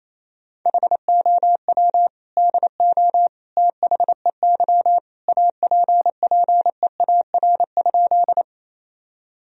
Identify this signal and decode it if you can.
Morse code, heard as a keyed tone, about 28 wpm, 700 Hz. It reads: HOW DO THEY APPEAR?
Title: Sentences from Top 1000 Words - Review Part 2 - 28wpm